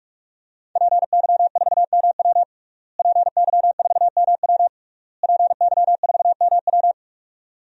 Morse code, heard as a keyed tone, about 46 wpm, 700 Hz. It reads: PY4MW PY4MW PY4MW
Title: DXCC-100 Call Signs - 5 Characters - Repeated 3x - 45wpm